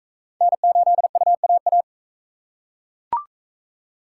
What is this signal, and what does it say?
Morse code, heard as a keyed tone, about 45 wpm, 700 Hz. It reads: N8UAA